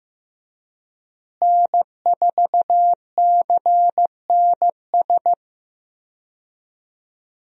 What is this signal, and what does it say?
Morse code, heard as a keyed tone, about 15 wpm, 700 Hz. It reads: N4CNS
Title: USA Call Signs - 5 Characters - Repeated 2x - 15wpm